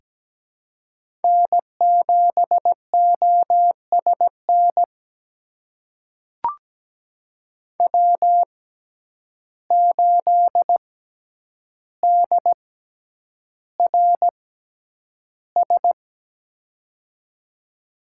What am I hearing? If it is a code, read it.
Morse code, heard as a keyed tone, about 17 wpm, spaced as for 9 wpm, 700 Hz. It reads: N7OSN W8DRS
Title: USA Call Signs - 5 Characters - 6x Character Spacing - 17wpm